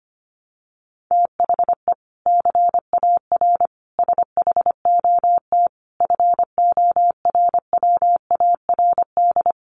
Morse code, heard as a keyed tone, about 25 wpm, 700 Hz. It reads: THE CAR SHOT FORWARD
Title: Sentences from Top 1300 Words 25wpm